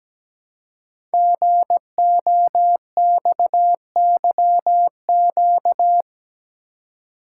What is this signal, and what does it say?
Morse code, heard as a keyed tone, about 17 wpm, 700 Hz. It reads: GOXYQ
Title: Code Groups - 5 Letter:Numbers - 17wpm